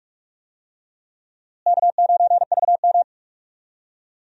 Morse code, heard as a keyed tone, about 46 wpm, 700 Hz. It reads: K9VM
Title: USA Call Signs - 4 Characters - Repeated 3x - 45wpm